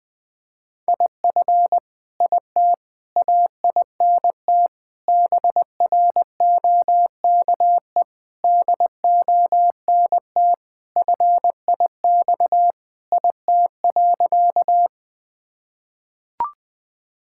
Morse code, heard as a keyed tone, about 21 wpm, 700 Hz. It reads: IF IT AINT BROKE DONT FIX IT.
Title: Sentences with 7-Word Phrases - 20wpm